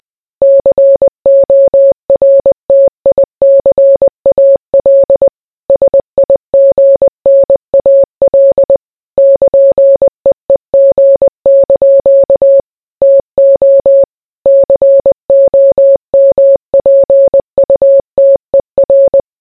Morse code, heard as a keyed tone, about 20 wpm, 550 Hz. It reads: CORTICAL SIGNAL <KN>EEG) TO COMPUTER